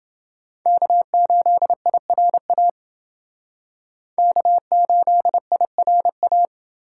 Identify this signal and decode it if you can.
Morse code, heard as a keyed tone, about 29 wpm, 700 Hz. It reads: K8IRA K8IRA